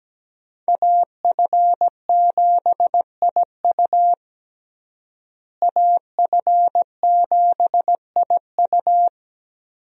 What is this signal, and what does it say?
Morse code, heard as a keyed tone, about 17 wpm, 700 Hz. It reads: AF7IU AF7IU
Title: USA Call Signs - 5 Characters - Repeated 2x - 17wpm